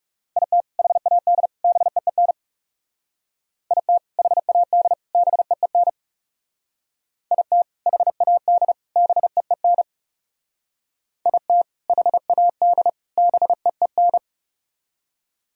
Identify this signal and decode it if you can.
Morse code, heard as a keyed tone, about 36 wpm, 700 Hz. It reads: IT HAD BEEN IT HAD BEEN IT HAD BEEN IT HAD BEEN
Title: Speed-Racing - Sets of 3 Words 30wpm